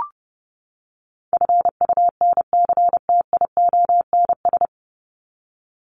Morse code, heard as a keyed tone, about 30 wpm, 700 Hz. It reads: FUNCTIONS
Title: ES:S Suffix - Top 100 Words with Base Words 30wpm